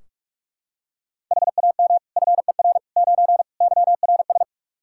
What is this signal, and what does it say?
Morse code, heard as a keyed tone, about 46 wpm, 700 Hz. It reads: HAM FER 9 YRS
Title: K1USN SST QSO Practice 45wpm